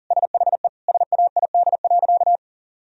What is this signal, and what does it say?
Morse code, heard as a keyed tone, about 40 wpm, 700 Hz. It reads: SHE SAID.
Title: Sentences from Top 1000 Words - Review Part 2 - 40wpm